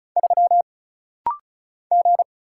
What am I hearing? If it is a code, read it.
Morse code, heard as a keyed tone, about 36 wpm, 700 Hz. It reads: 3 G
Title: Single Letter-Number - Rapid-Fire 35wpm